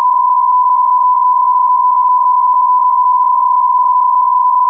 PSK63, about 62.5 baud.